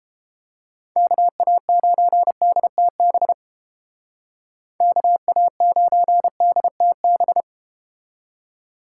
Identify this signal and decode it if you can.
Morse code, heard as a keyed tone, about 33 wpm, 700 Hz. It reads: KA9DTB KA9DTB